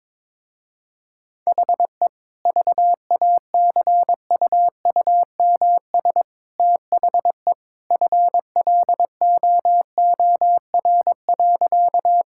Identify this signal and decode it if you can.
Morse code, heard as a keyed tone, about 21 wpm, 700 Hz. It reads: HE VACUUMS THE FLOOR.
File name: practice / 22wpm / Easy Sentences with Hints - 22wpm